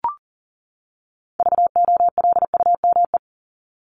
Morse code, heard as a keyed tone, about 40 wpm, 700 Hz. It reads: VOLUME
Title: Top 1751-2000 Words 40wpm